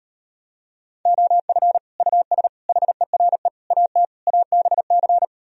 Morse code, heard as a keyed tone, about 39 wpm, 700 Hz. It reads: OF US HERE AT ABC